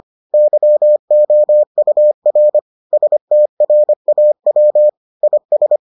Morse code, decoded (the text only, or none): YOUR STRAW IS